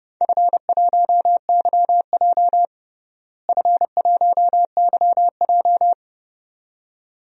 Morse code, heard as a keyed tone, about 30 wpm, 700 Hz. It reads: F1YJ F1YJ
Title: DXCC-100 Call Signs - 4 Characters - Repeated 2x - 30wpm